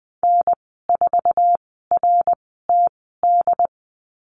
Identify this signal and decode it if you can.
Morse code, heard as a keyed tone, about 20 wpm, spaced as for 14 wpm, 700 Hz. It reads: N4RTD
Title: 069 Lesson 12g - Letter d - Call Signs 20wpm